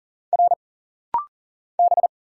Morse code, heard as a keyed tone, about 40 wpm, 700 Hz. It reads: R B
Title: Single Letters - Rapid-Fire 40wpm